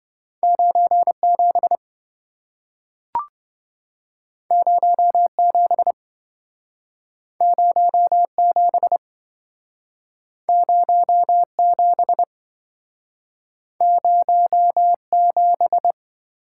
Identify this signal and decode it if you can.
Morse code, heard as a keyed tone, about 27 wpm, 700 Hz. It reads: 97 07 07 07 07